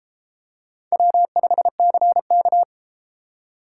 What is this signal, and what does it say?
Morse code, heard as a keyed tone, about 33 wpm, 700 Hz. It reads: W5CK